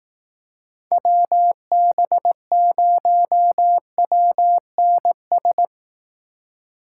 Morse code, heard as a keyed tone, about 18 wpm, 700 Hz. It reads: WB0WNS
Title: Speed Racing - USA 2x3 Call Signs 15wpm